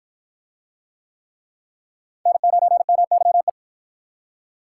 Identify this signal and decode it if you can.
Morse code, heard as a keyed tone, about 53 wpm, 700 Hz. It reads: N9MYE